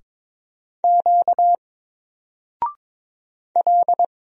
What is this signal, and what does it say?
Morse code, heard as a keyed tone, about 22 wpm, 700 Hz. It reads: Q L
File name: Single-Letter-Number-22wpm